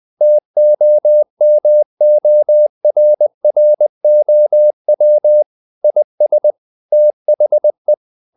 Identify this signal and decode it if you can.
Morse code, heard as a keyed tone, about 20 wpm, 600 Hz. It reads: TOMORROW IS THE
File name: the old man and the sea / the old man and the sea part 1 20wpm